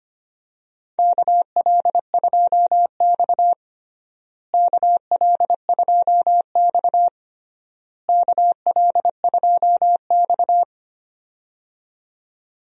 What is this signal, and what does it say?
Morse code, heard as a keyed tone, about 25 wpm, 700 Hz. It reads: KL2X KL2X KL2X
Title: USA Call Signs - 4 Characters - Repeated 3x - 25wpm